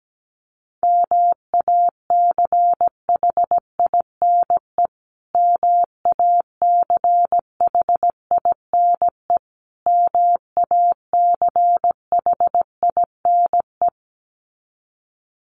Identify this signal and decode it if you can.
Morse code, heard as a keyed tone, about 17 wpm, 700 Hz. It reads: MACHINE MACHINE MACHINE
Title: Top 500 Words Repeated 17wpm